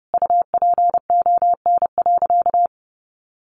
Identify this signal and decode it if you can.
Morse code, heard as a keyed tone, about 30 wpm, 700 Hz. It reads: UPON.
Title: Sentences from Top 1400 Words 30wpm